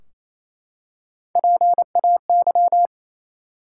Morse code, heard as a keyed tone, about 28 wpm, 700 Hz. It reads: PAY
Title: Three Letter Words - 28wpm